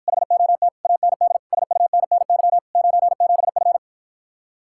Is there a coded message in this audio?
Yes: Morse code from 0.1 to 3.8 s, heard as a keyed tone, about 54 wpm, 700 Hz. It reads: HOT AND SUNNY 97F